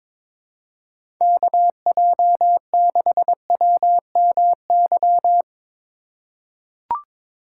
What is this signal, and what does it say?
Morse code, heard as a keyed tone, about 22 wpm, 700 Hz. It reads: KJ6WMY
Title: USA Call Signs - 2x Character Spacing - 22wpm